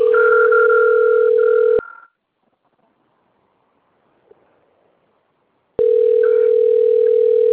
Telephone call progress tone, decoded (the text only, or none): ringback tone